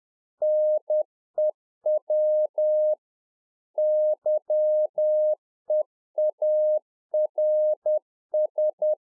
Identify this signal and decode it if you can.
Morse code, heard as a keyed tone, about 10 wpm, 610 Hz. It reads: NEW YEARS
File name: 2011-04-mp3 / A SMALL TRANCEIVER 1 10WPM